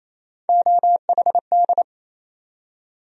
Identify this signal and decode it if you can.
Morse code, heard as a keyed tone, about 28 wpm, 700 Hz. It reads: OHD
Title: USA Call Sign Postfixes - 3 Letters 28wpm